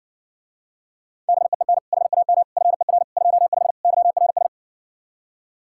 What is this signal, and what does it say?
Morse code, heard as a keyed tone, about 60 wpm, 700 Hz. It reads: BEEN HAM FER 25 YRS